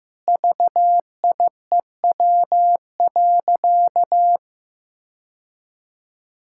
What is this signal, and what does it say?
Morse code, heard as a keyed tone, about 15 wpm, 700 Hz. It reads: VIEW.
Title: Sentences from Top 1600 Words 15wpm